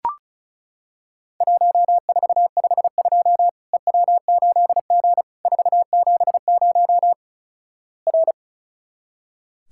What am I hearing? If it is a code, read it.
Morse code, heard as a keyed tone, about 36 wpm, 700 Hz. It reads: 1452 EW8G 470 R
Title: Worked All Europe DX Contest - QTC Exchange - 35wpm